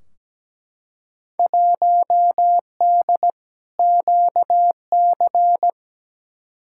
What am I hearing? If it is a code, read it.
Morse code, heard as a keyed tone, about 17 wpm, 700 Hz. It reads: 1D QC
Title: Field Day Exchanges - 17wpm